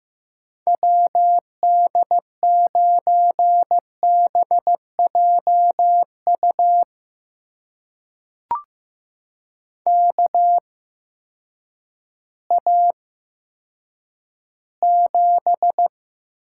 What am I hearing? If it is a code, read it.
Morse code, heard as a keyed tone, about 15 wpm, spaced as for 9 wpm, 700 Hz. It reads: WD9BJU KA7
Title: USA Call Signs - 6 Characters - 8x Character Spacing - 15wpm